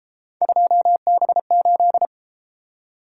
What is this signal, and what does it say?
Morse code, heard as a keyed tone, about 32 wpm, 700 Hz. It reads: 2B8